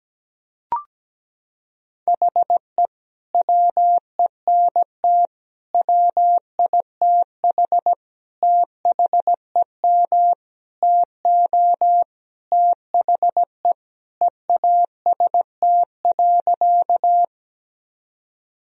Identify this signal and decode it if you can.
Morse code, heard as a keyed tone, about 17 wpm, 700 Hz. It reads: HE WENT WITH THEM TO THE EAST.
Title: Sentences from Top 1000 Words - Review Part 3 - 17wpm